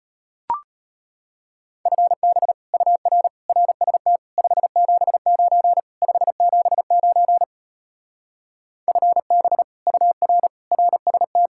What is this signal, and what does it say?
Morse code, heard as a keyed tone, about 33 wpm, 700 Hz. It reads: FB UR RST 579 579 FB UR RST